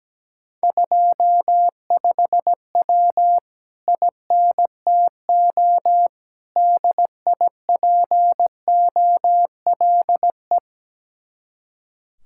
Morse code, heard as a keyed tone, about 17 wpm, 700 Hz. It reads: 25W INTO DIPOLE
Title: Speed-Racing Common QSO Phrases - 17wpm